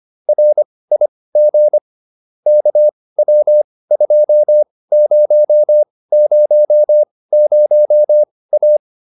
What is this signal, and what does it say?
Morse code, heard as a keyed tone, about 25 wpm, spaced as for 18 wpm, 600 Hz. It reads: RIG KW2000A
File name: QSO410 18